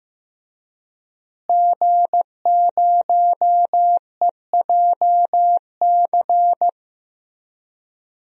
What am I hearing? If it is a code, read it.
Morse code, heard as a keyed tone, about 15 wpm, 700 Hz. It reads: G0EJC
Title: DXCC-100 Call Signs - 5 Characters - Repeated 3x - 15wpm